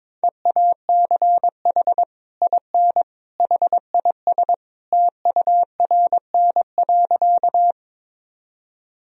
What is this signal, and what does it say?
Morse code, heard as a keyed tone, about 22 wpm, 700 Hz. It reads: EACH IN HIS TURN.
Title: Sentences from Top 1000 Words - Review Part 1 - 22wpm